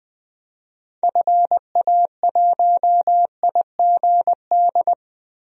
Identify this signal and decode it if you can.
Morse code, heard as a keyed tone, about 20 wpm, 700 Hz. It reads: FA1IGD
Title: DXCC-100 Call Signs - 6 Characters - 20wpm